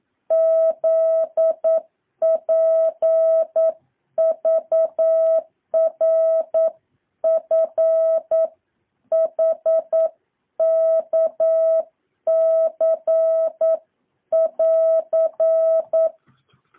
Morse code, heard as a keyed tone, about 9 wpm, 650 Hz. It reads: Z P V R F H K C <AR>
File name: jh8cbh / 2023cw